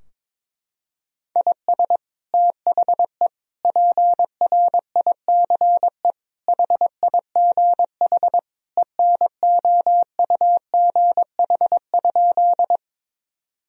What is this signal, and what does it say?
Morse code, heard as a keyed tone, about 22 wpm, 700 Hz. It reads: IS THE PRICE HIGH ENOUGH?
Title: Sentences from Top 1000 Words - Review Part 4 - 22wpm